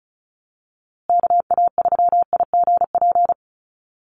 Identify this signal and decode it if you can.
Morse code, heard as a keyed tone, about 34 wpm, 700 Hz. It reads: KA3IGP
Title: USA Call Signs Repeated 2x 35wpm